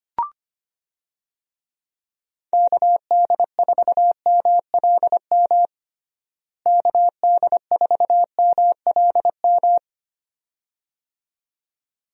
Morse code, heard as a keyed tone, about 25 wpm, 700 Hz. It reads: KD4MLM KD4MLM
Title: USA Call Signs - 6 Characters - Repeated 2x - 25wpm